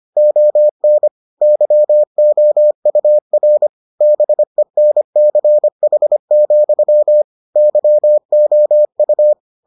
Morse code, heard as a keyed tone, about 25 wpm, 600 Hz. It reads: ON YOUR BENCH, YOU